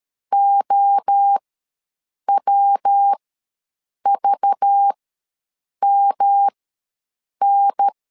Morse code, heard as a keyed tone, about 13 wpm, spaced as for 7 wpm, 800 Hz. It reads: OWVMN
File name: LEZIONI / 6 PKX0 / 6 PKX0 ws 12wpm 3448